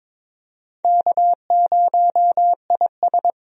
Morse code, heard as a keyed tone, about 22 wpm, 700 Hz. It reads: K0IS
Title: USA Call Signs - 4 Characters - 22wpm